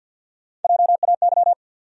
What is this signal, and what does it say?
Morse code, heard as a keyed tone, about 52 wpm, 700 Hz. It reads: JAY
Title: USA Call Sign Postfixes - 3 Letters 50wpm